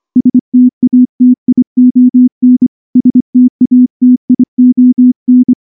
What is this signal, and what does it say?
Morse code, heard as a keyed tone, about 24 wpm, 250 Hz. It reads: STATION STATION
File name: morse / morse-training-25WPM